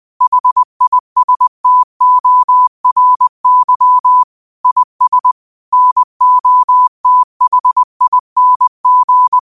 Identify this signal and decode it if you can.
Morse code, heard as a keyed tone, about 20 wpm, 1000 Hz. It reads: HISTORY IS NOTHING
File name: qotdfiles / 20wpm